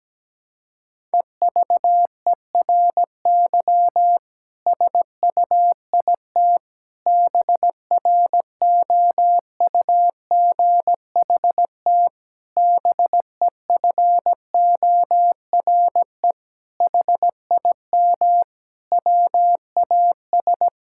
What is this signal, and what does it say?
Morse code, heard as a keyed tone, about 17 wpm, 700 Hz. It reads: EVERY SUIT BROUGHT BEFORE HIM WAS
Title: Sentences from Top 1600 Words 17wpm